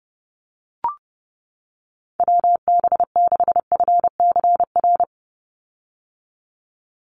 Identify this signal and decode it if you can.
Morse code, heard as a keyed tone, about 30 wpm, 700 Hz. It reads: WB6FCR
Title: USA Call Signs 30wpm